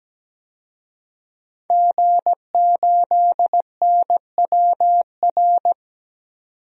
Morse code, heard as a keyed tone, about 17 wpm, 700 Hz. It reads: G8NWR
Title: DXCC-100 Call Signs - 5 Characters - 17wpm